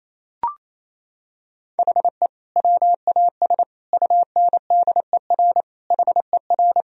Morse code, heard as a keyed tone, about 28 wpm, 700 Hz. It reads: HE WAS UNDER HER
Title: Sentences from Top 1000 Words - Review Part 1 - 28wpm